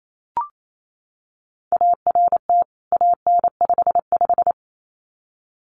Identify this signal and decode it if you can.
Morse code, heard as a keyed tone, about 28 wpm, 700 Hz. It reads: ART AN55
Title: CWT Exchange 28wpm